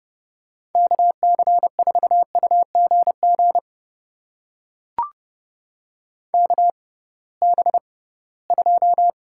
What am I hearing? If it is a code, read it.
Morse code, heard as a keyed tone, about 31 wpm, spaced as for 20 wpm, 700 Hz. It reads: KC4UGG KB2
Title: USA Call Signs - 6 Characters - 6x Character Spacing - 30wpm